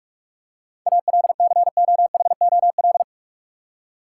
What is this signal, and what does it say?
Morse code, heard as a keyed tone, about 41 wpm, 700 Hz. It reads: ALCOHOL
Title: Seven Letter Words - 45wpm